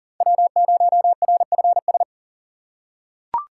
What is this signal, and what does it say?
Morse code, heard as a keyed tone, about 40 wpm, 700 Hz. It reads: W0RFS
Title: USA Call Signs - 5 Characters - 6x Character Spacing - 40wpm